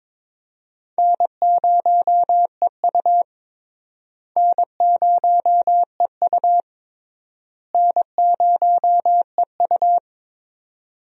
Morse code, heard as a keyed tone, about 22 wpm, 700 Hz. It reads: N0EU N0EU N0EU
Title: USA Call Signs - 4 Characters - Repeated 3x - 22wpm